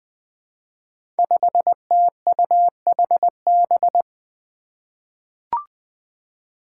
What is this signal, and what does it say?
Morse code, heard as a keyed tone, about 20 wpm, 700 Hz. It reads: 5TUHB
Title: Five Challenging Letter-Number Combinations 20wpm